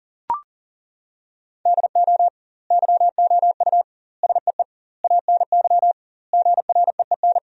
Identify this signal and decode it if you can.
Morse code, heard as a keyed tone, about 41 wpm, 700 Hz. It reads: DO YOU SEE ANY GREEN
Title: 2x Word-Spacing - Sentences from Top 500 Words - 40wpm